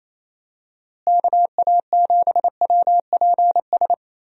Morse code, heard as a keyed tone, about 28 wpm, 700 Hz. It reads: KA7WPS
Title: USA Call Signs - 6 Characters - Repeated 2x - 28wpm